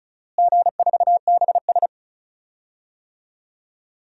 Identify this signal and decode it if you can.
Morse code, heard as a keyed tone, about 36 wpm, 700 Hz. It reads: G4BS
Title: DXCC-100 Call Signs - 4 Characters - Repeated 3x - 35wpm